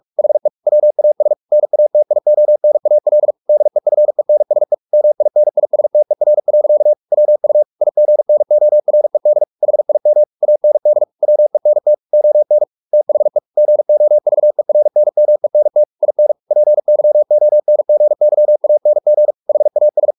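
Morse code, heard as a keyed tone, about 44 wpm, 600 Hz. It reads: HE WAS NATIONAL DEFENSE MINISTER. WU IGNORED HIM AND WENT ON THE GOVERNMENT IN PYONGYANG HAS
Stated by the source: Morse practice at 45 wpm, the novel Never by Ken Follett in English, from CW med Gnister.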